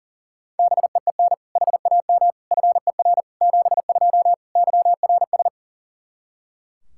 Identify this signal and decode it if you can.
Morse code, heard as a keyed tone, about 40 wpm, 700 Hz. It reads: BEEN HAM FER 72 YRS
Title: Practice QSO - Exchange 3 - More Info - 40wpm